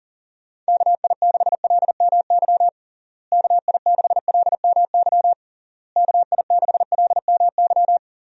Morse code, heard as a keyed tone, about 40 wpm, 700 Hz. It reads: KI6LMY KI6LMY KI6LMY
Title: USA Call Signs - 6 Characters - Repeated 3x - 40wpm